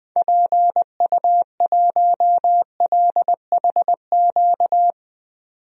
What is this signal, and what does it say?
Morse code, heard as a keyed tone, about 20 wpm, 700 Hz. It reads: PU1LHQ
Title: DXCC-100 Call Signs - 6 Characters - 20wpm